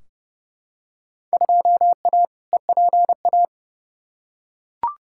Morse code, heard as a keyed tone, about 30 wpm, 700 Hz. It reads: 2A EPA
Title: Field Day Exchanges - 30wpm